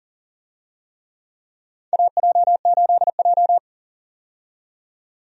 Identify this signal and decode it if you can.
Morse code, heard as a keyed tone, about 37 wpm, 700 Hz. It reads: AJ8J
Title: USA Call Signs - 4 Characters - 40wpm